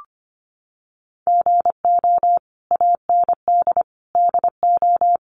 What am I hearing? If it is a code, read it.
Morse code, heard as a keyed tone, about 25 wpm, 700 Hz. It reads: GO AND DO
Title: Sentences from Top 1100 Words 25wpm